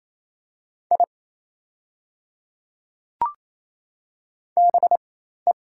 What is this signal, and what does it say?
Morse code, heard as a keyed tone, about 28 wpm, spaced as for 13 wpm, 700 Hz. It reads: I BE
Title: 4x Character Spacing - Top 100 Words - 28wpm